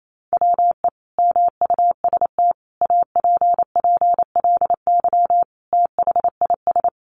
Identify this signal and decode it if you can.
Morse code, heard as a keyed tone, about 28 wpm, 700 Hz. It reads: WE MUST APPLY THIS
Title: Sentences from Top 1400 Words 28wpm